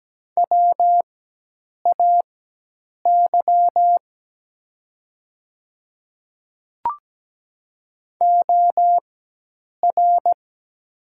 Morse code, heard as a keyed tone, about 17 wpm, spaced as for 8 wpm, 700 Hz. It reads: WAY OR
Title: 4x Character Spacing - Top 100 Words - 17wpm